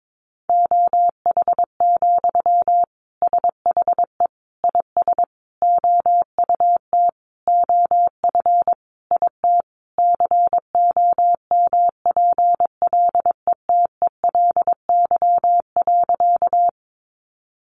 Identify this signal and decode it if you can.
Morse code, heard as a keyed tone, about 22 wpm, 700 Hz. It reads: OH, SHE IS OUT OF IT COMPLETELY.
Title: Sentences from Top 1100 Words 22wpm